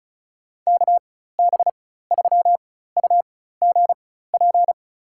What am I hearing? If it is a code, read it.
Morse code, heard as a keyed tone, about 34 wpm, spaced as for 16 wpm, 700 Hz. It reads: KB3UGP
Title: USA Call Signs - 6 Characters - 4x Character Spacing - 35wpm